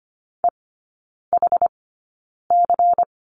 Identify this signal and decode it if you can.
Morse code, heard as a keyed tone, about 25 wpm, spaced as for 9 wpm, 700 Hz. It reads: EHC
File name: course / 25:10wpm / 093 Lesson 16e - Letter c - Three Characters 25wpm